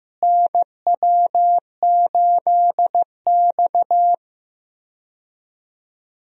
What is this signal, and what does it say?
Morse code, heard as a keyed tone, about 15 wpm, 700 Hz. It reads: NW8X
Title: USA Call Signs - 4 Characters - Repeated 3x - 15wpm